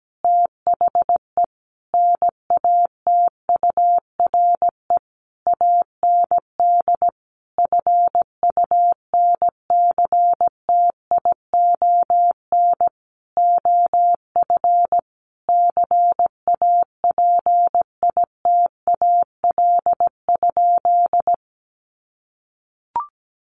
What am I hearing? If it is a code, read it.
Morse code, heard as a keyed tone, about 17 wpm, 700 Hz. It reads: THE NATURE AND FUNCTION OF CAPITAL?
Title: Sentences from Top 1100 Words 17wpm